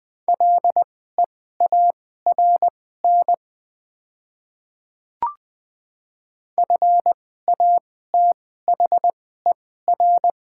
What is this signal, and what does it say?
Morse code, heard as a keyed tone, about 21 wpm, spaced as for 15 wpm, 700 Hz. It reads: LEARN FATHER
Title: 2x Character Spacing - Top 300 Words - 20wpm